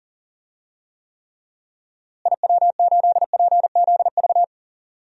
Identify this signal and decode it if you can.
Morse code, heard as a keyed tone, about 37 wpm, 700 Hz. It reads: IW8PZV
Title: DXCC-100 Call Signs - 6 Characters - Repeated 3x - 40wpm